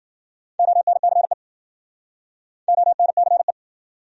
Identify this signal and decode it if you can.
Morse code, heard as a keyed tone, about 70 wpm, 700 Hz. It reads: ONCE ONCE